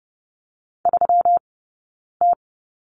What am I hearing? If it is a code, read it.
Morse code, heard as a keyed tone, about 30 wpm, 700 Hz. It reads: 3 T
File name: Lesson 20d - Number 3 - Two Characters 30wpm